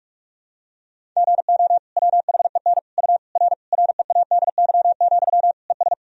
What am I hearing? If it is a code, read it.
Morse code, heard as a keyed tone, about 42 wpm, 700 Hz. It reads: GO WHEN U R READY, ES
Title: Sentences from Top 500 Words and Abbreviations 45wpm